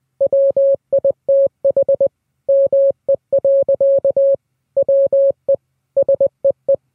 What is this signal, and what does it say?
Morse code, heard as a keyed tone, about 20 wpm, 550 Hz. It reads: WITH ME. WE SEE